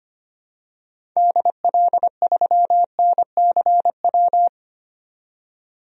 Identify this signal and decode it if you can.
Morse code, heard as a keyed tone, about 25 wpm, 700 Hz. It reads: DL3NCW
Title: DXCC-100 Call Signs - 6 Characters - 25wpm